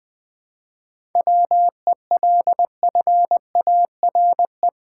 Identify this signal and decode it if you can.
Morse code, heard as a keyed tone, about 20 wpm, 700 Hz. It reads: WELFARE
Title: 7 Letter Words - First 3 Letters Given - 20wpm